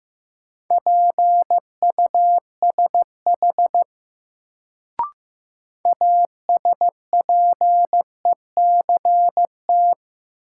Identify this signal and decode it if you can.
Morse code, heard as a keyed tone, about 15 wpm, 700 Hz. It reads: PUSH ASPECT
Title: Sentences from Top 1300 Words - New Word Drill - 15wpm